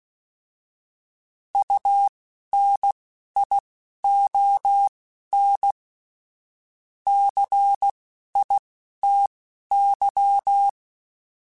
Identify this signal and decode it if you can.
Morse code, heard as a keyed tone, about 16 wpm, spaced as for 11 wpm, 800 Hz. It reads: UNION CITY